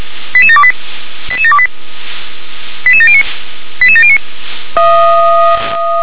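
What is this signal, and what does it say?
Five-tone selective call (ZVEI), heard as quick runs of five tones, tones of about 70 ms. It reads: 88728 88728 88709 88709
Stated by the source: ZVEI five-tone call 88730, received by SDR.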